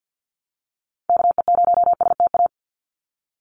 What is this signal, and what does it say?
Morse code, heard as a keyed tone, about 48 wpm, 700 Hz. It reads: KE0STA
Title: USA Call Signs 50wpm